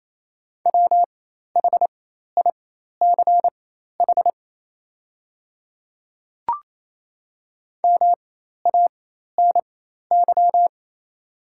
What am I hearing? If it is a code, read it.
Morse code, heard as a keyed tone, about 28 wpm, spaced as for 13 wpm, 690 Hz. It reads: WHICH MANY
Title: 4x Character Spacing - Top 100 Words - 28wpm